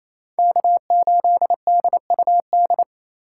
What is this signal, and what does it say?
Morse code, heard as a keyed tone, about 28 wpm, 700 Hz. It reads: K8DUD
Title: USA Call Signs - 5 Characters - 2x Character Spacing - 28wpm